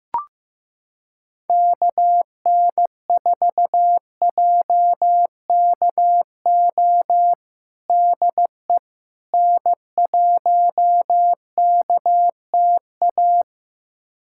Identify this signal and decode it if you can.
Morse code, heard as a keyed tone, about 15 wpm, 700 Hz. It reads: KN4JKO DE N1KTA
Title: USA Callsign DE Callsign 15wpm